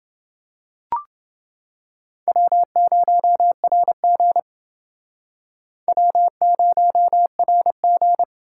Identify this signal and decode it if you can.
Morse code, heard as a keyed tone, about 28 wpm, 700 Hz. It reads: W0RG W0RG